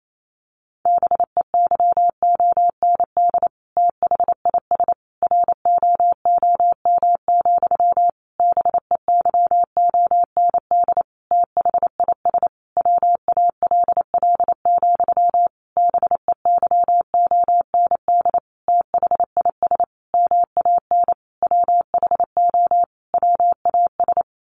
Morse code, heard as a keyed tone, about 28 wpm, 700 Hz. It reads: BEYOND THIS ROOM, BEYOND THIS WALL, BEYOND THIS MAN WHO WAS
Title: A Sound Of Thunder 28wpm